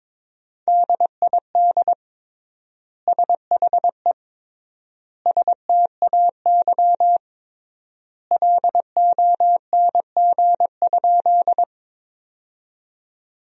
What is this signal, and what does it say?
Morse code, heard as a keyed tone, about 22 wpm, 700 Hz. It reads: DID SHE STAY LONG?
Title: 3x Word-Spacing - Sentences from Top 500 Words - 22wpm